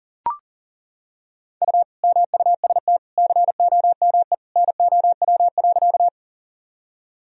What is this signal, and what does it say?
Morse code, heard as a keyed tone, about 40 wpm, 700 Hz. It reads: U MUST COME NOW.
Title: Sentences from Top 200 Words and Abbreviations 40wpm